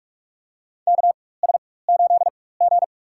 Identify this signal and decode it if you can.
Morse code, heard as a keyed tone, about 46 wpm, spaced as for 21 wpm, 700 Hz. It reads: KS8G
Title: USA Call Signs - 4 Characters - 4x Character Spacing - 45wpm